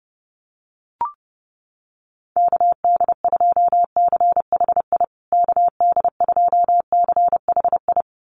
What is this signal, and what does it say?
Morse code, heard as a keyed tone, about 30 wpm, 700 Hz. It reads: KD2CHI KD2CHI
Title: USA Call Signs Repeated 2x 30wpm